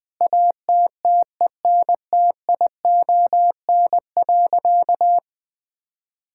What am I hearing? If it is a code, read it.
Morse code, heard as a keyed tone, about 20 wpm, 700 Hz. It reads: ATTENTION.
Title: Sentences from Top 1000 Words - Review Part 6 - 20wpm